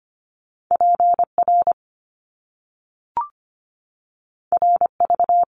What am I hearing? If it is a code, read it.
Morse code, heard as a keyed tone, about 25 wpm, 700 Hz. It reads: PR RV
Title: Top Two Letter Combinations in Words 25wpm